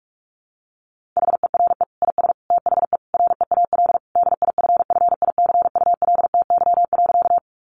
Morse code, heard as a keyed tone, about 44 wpm, 700 Hz. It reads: HERE IS THE REAL DIFFICULTY.
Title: Sentences from Top 1500 Words 45wpm